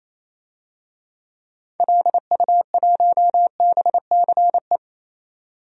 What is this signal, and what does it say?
Morse code, heard as a keyed tone, about 28 wpm, 700 Hz. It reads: LU1BCE
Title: DXCC-100 Call Signs - 6 Characters - 28wpm